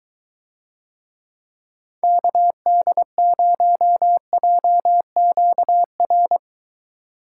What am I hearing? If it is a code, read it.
Morse code, heard as a keyed tone, about 23 wpm, 700 Hz. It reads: KD0JQR